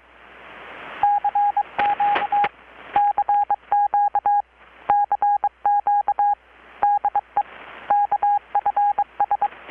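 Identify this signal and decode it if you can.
Morse code, heard as a keyed tone, about 22 wpm, 820 Hz, with radio noise. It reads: CQ CQ CQ DE KFS
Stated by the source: Morse from KPH's Night of Nights XX maritime radiotelegraphy event, received in Chiapas, Mexico about 3,800 km away on a Kenwood R-600 with a grounded 10 m long-wire antenna.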